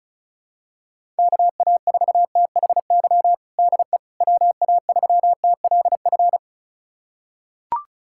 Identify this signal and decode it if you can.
Morse code, heard as a keyed tone, about 36 wpm, 700 Hz. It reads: KA4THY DE WA3TLF
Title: USA Callsign DE Callsign 35wpm